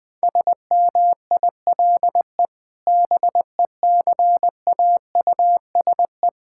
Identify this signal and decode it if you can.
Morse code, heard as a keyed tone, about 21 wpm, 700 Hz. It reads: SMILE BECAUSE
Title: Sentences with 2-Word Phrases - Spoken Beginning - 20wpm